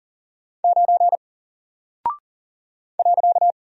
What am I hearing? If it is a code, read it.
Morse code, heard as a keyed tone, about 38 wpm, 700 Hz. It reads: 9 .
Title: Single Character (Letter:Number:Punctuation:Prosign) Alt 40wpm